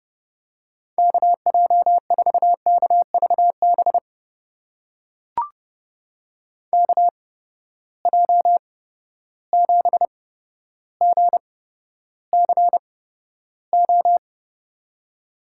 Morse code, heard as a keyed tone, about 32 wpm, spaced as for 13 wpm, 700 Hz. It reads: KJ4KVB KJ7GCO
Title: USA Call Signs - 6 Characters - 8x Character Spacing - 30wpm